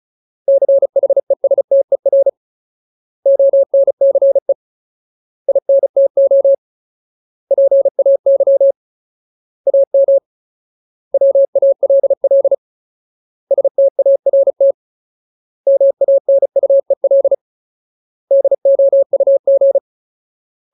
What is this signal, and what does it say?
Morse code, heard as a keyed tone, about 36 wpm, 550 Hz. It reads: CHESTER ONCE INTO PAY AM WALL START MANUEL DOUG
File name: Ultimate Word List / 35WPM Ultimate